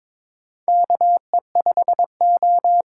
Morse code, heard as a keyed tone, about 22 wpm, 700 Hz. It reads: KE5O
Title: USA Call Signs - 8x Character Spacing - 22wpm